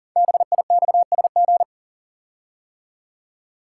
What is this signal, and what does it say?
Morse code, heard as a keyed tone, about 38 wpm, 690 Hz. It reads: BIXSG
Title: Code Groups - 5 Letters - 40wpm